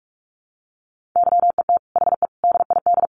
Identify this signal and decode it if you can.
Morse code, heard as a keyed tone, about 44 wpm, 700 Hz. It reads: YET HE DID
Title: Sentences from Top 800 Words 45wpm